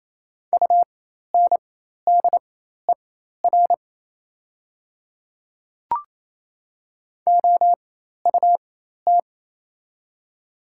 Morse code, heard as a keyed tone, about 27 wpm, spaced as for 13 wpm, 700 Hz. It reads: UNDER OUT
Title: 4x Character Spacing - Top 200 Words - 28wpm